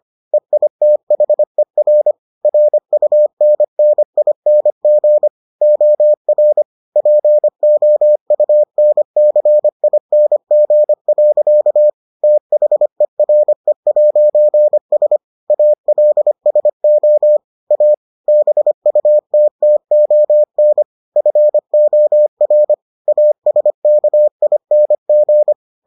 Morse code, heard as a keyed tone, about 25 wpm, 600 Hz. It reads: EITHER RUNNING OR POUNCING. THERE'S ALSO A BUTTON FOR ASKING